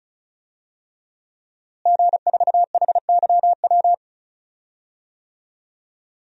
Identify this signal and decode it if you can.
Morse code, heard as a keyed tone, about 36 wpm, 700 Hz. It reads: G4HYW
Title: DXCC-100 Call Signs - 5 Characters - Repeated 3x - 35wpm